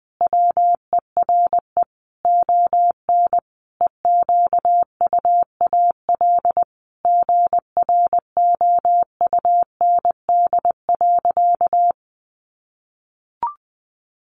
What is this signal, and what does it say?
Morse code, heard as a keyed tone, about 20 wpm, 700 Hz. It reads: WERE ON EQUAL GROUND.
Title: Sentences from Top 1400 Words 20wpm